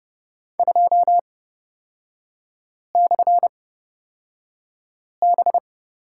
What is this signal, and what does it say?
Morse code, heard as a keyed tone, about 30 wpm, spaced as for 5 wpm, 700 Hz. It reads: 2/B
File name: Lesson 38e - Letter q - Three Characters 30wpm